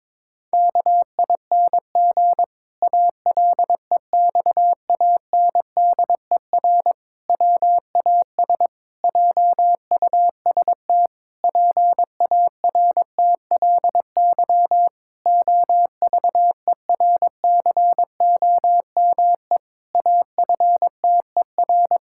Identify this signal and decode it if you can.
Morse code, heard as a keyed tone, about 22 wpm, 700 Hz. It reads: KING ALEXANDER WAS JUST PARTLY OVERCOME AFTER
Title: Pangram Challenge v2 - 22wpm